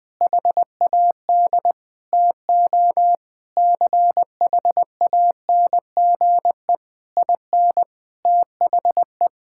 Morse code, heard as a keyed tone, about 20 wpm, 700 Hz. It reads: HAD TO CHANGE IN THE
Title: All-Age Appropriate Jokes - 20wpm